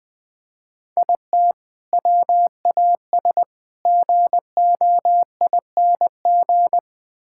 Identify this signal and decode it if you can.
Morse code, heard as a keyed tone, about 20 wpm, 700 Hz. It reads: IT WAS GOING
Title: Build A Phrase - 5-Word Phrases - 20wpm